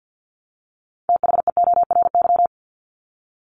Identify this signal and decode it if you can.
Morse code, heard as a keyed tone, about 50 wpm, 700 Hz. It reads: THEORY
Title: Top 501-750 Words 50wpm